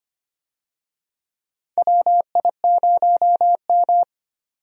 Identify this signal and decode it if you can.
Morse code, heard as a keyed tone, about 25 wpm, 700 Hz. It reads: WI0M
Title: USA Call Signs - 4 Characters - 25wpm